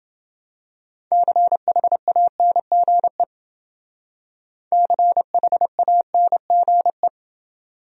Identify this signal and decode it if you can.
Morse code, heard as a keyed tone, about 29 wpm, 700 Hz. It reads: CHANGE CHANGE